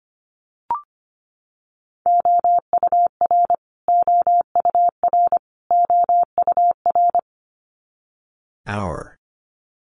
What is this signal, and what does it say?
Morse code, heard as a keyed tone, about 25 wpm, 700 Hz. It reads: OUR OUR OUR
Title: Top 300 Words Repeated 25wpm